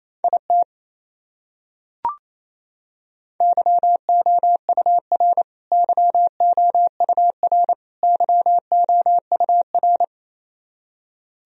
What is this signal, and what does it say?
Morse code, heard as a keyed tone, about 28 wpm, 700 Hz. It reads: IT YOUR YOUR YOUR
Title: Top 100 Words Repeated 28wpm